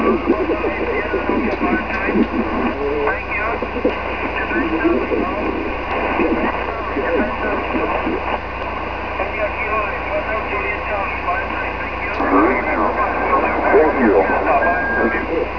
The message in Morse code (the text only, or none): none